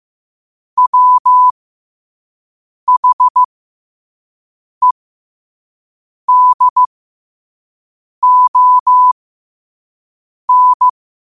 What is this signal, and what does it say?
Morse code, heard as a keyed tone, about 15 wpm, spaced as for 5 wpm, 1000 Hz. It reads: WHEDON
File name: qotdfiles / 5wpm